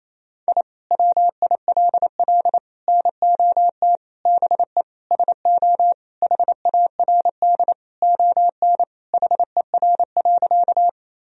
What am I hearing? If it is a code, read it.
Morse code, heard as a keyed tone, about 28 wpm, 700 Hz. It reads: I WILL NOT BE SO HARD ON HER.
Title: Sentences from Top 1000 Words - Review Part 1 - 28wpm